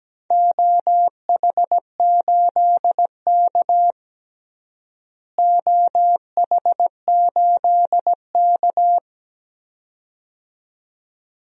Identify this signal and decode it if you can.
Morse code, heard as a keyed tone, about 17 wpm, 700 Hz. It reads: OH8K OH8K
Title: DXCC-100 Call Signs - 4 Characters - Repeated 2x - 17wpm